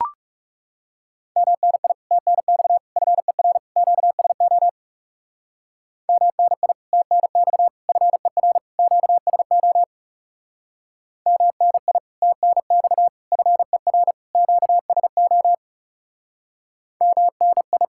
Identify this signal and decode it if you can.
Morse code, heard as a keyed tone, about 38 wpm, 700 Hz. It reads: MNI TNX FER QSO MNI TNX FER QSO MNI TNX FER QSO MNI